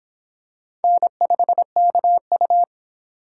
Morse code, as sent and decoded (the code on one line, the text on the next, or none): -. ..... -.- ..-
N5KU